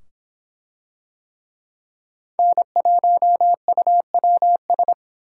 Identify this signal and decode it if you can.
Morse code, heard as a keyed tone, about 24 wpm, 700 Hz. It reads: N1UWS